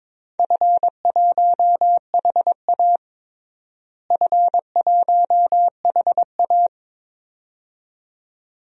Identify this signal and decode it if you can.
Morse code, heard as a keyed tone, about 22 wpm, 700 Hz. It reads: F1HA F1HA
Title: DXCC-100 Call Signs - 4 Characters - Repeated 2x - 22wpm